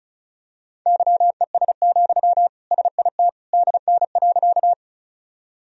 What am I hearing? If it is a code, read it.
Morse code, heard as a keyed tone, about 34 wpm, 700 Hz. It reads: YES, SIT DN.
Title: Sentences from Top 500 Words and Abbreviations 35wpm